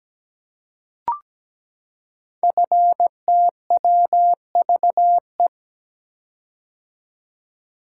Morse code, heard as a keyed tone, about 17 wpm, 700 Hz. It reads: FTWVE
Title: Code Groups - 5 Letters - 17wpm